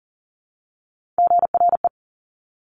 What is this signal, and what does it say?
Morse code, heard as a keyed tone, about 38 wpm, 700 Hz. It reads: GRE
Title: Top Three Letter Combinations in Words 40wpm